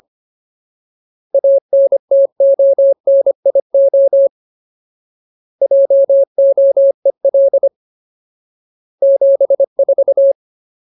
Morse code, heard as a keyed tone, about 25 wpm, 550 Hz. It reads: ANTONIO JOEL 74
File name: Ultimate Word List / 25WPM Ultimate